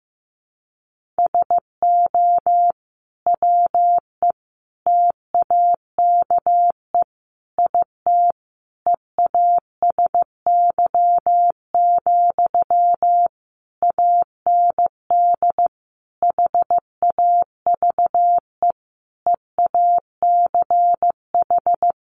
Morse code, heard as a keyed tone, about 14 wpm, 700 Hz. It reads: SO WE TAKE IT EASY, AND HAVE EACH
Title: Sentences from Top 700 Words 15wpm